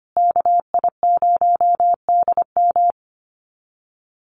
Morse code, heard as a keyed tone, about 25 wpm, 700 Hz. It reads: KI0DM
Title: USA Call Signs Repeated 3x 25wpm